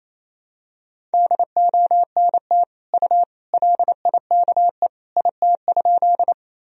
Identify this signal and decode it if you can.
Morse code, heard as a keyed tone, about 28 wpm, 700 Hz. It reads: DONT U LIKE IT?
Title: Sentences from Top 300 Words and Abbreviations 28wpm